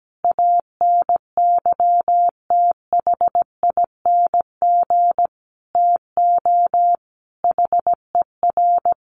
Morse code, heard as a keyed tone, about 17 wpm, 700 Hz. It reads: ANYTHING TO HER
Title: Sentences from Top 1000 Words 17wpm